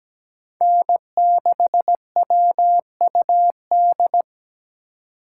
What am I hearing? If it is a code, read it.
Morse code, heard as a keyed tone, about 20 wpm, 700 Hz. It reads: N6WUD
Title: Speed Racing - USA 1x3 Call Signs 17wpm